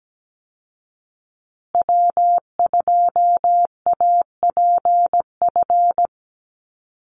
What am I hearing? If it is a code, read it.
Morse code, heard as a keyed tone, about 17 wpm, 700 Hz. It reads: W2APF
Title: Call Signs Repeated 2x - 17wpm